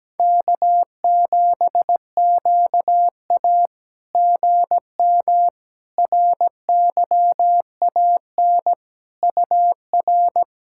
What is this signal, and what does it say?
Morse code, heard as a keyed tone, about 17 wpm, 700 Hz. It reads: K7QA GM RYAN UR
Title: SOTA QSO Practice 17wpm